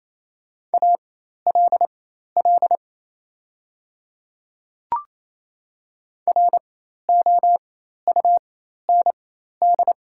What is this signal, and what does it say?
Morse code, heard as a keyed tone, about 28 wpm, spaced as for 13 wpm, 700 Hz. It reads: ALL ROUND
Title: 4x Character Spacing - Top 200 Words - 28wpm